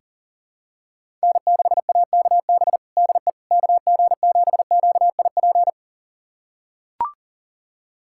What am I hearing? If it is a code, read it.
Morse code, heard as a keyed tone, about 37 wpm, 700 Hz. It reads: N6AKB DE KG7QIP
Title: USA Callsign DE Callsign 40wpm